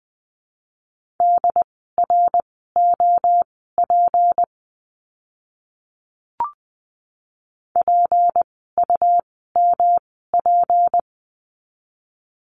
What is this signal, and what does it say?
Morse code, heard as a keyed tone, about 20 wpm, spaced as for 14 wpm, 700 Hz. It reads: DROP PUMP
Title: 153 Lesson 25f - Letter p - Words 20wpm